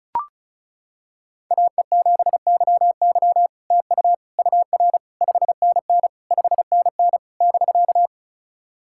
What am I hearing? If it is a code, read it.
Morse code, heard as a keyed tone, about 36 wpm, 700 Hz. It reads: AE7YY TU UR 5NN 5NN <BK>
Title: SOTA QSO Practice 35wpm